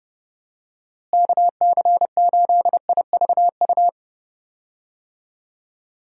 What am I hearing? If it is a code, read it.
Morse code, heard as a keyed tone, about 30 wpm, 700 Hz. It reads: KC8IVU
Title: USA Call Signs - 6 Characters - Repeated 3x - 30wpm